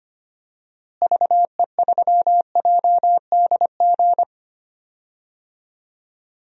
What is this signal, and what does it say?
Morse code, heard as a keyed tone, about 25 wpm, 700 Hz. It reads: VE3JDG
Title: DXCC-100 Call Signs - 6 Characters - Repeated 3x - 25wpm